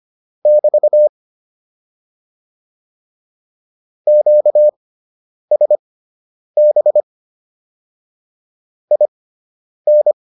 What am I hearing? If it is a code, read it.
Morse code, heard as a keyed tone, about 25 wpm, spaced as for 9 wpm, 600 Hz. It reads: <BT> QSB IN